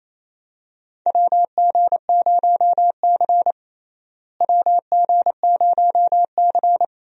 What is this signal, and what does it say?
Morse code, heard as a keyed tone, about 29 wpm, 700 Hz. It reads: WG0C WG0C
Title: USA Call Signs - 4 Characters - Repeated 2x - 28wpm